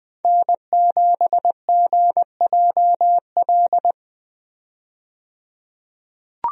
Morse code, heard as a keyed tone, about 20 wpm, 700 Hz. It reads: N7GJL
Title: USA Call Signs - 5 Characters - 20wpm